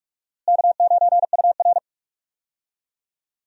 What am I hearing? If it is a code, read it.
Morse code, heard as a keyed tone, about 41 wpm, 700 Hz. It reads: K9UR
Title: USA Call Signs - 4 Characters - Repeated 2x - 45wpm